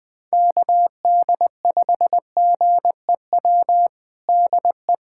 Morse code, heard as a keyed tone, about 20 wpm, 700 Hz. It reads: KD5GEW DE
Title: USA Callsign DE Callsign - 6 Characters - 20wpm